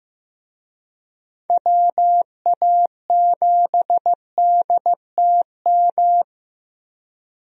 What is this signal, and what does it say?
Morse code, heard as a keyed tone, about 15 wpm, 700 Hz. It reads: WA7DTM
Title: USA Call Signs - 6 Characters - 15wpm